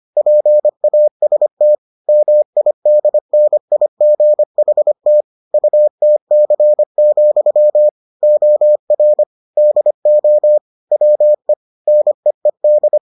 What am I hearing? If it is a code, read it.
Morse code, heard as a keyed tone, about 25 wpm, 600 Hz. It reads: PAST MIDNIGHT UTC, OR DO WE NEED